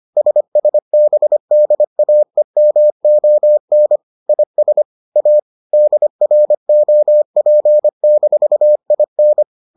Morse code, heard as a keyed tone, about 25 wpm, 600 Hz. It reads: SSBDAEMON IS A DROP-IN